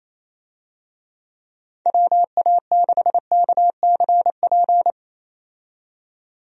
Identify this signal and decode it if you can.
Morse code, heard as a keyed tone, about 28 wpm, 700 Hz. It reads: WA6KCP
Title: USA Call Signs - 6 Characters - 28wpm